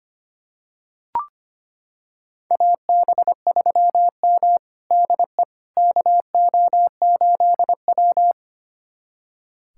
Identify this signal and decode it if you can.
Morse code, heard as a keyed tone, about 25 wpm, 700 Hz. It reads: AB3M DE KO8W
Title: USA Callsign DE Callsign - 4 Characters - 25wpm